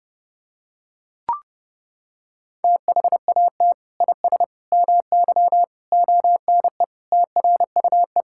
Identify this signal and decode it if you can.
Morse code, heard as a keyed tone, about 31 wpm, 700 Hz. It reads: THAT IS MY ONE TRUE
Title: Sentences from Top 1000 Words - Review Part 5 - 30wpm